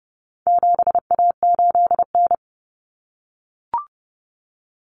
Morse code, heard as a keyed tone, about 30 wpm, 700 Hz. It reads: 7A8N
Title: Common Cut Numbers - 4-Digit 30wpm